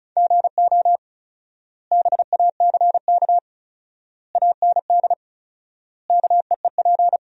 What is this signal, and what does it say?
Morse code, heard as a keyed tone, about 36 wpm, 700 Hz. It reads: GO BACK AND KEEP
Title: 4x Word-Spacing - Sentences from Top 300 Words - 35wpm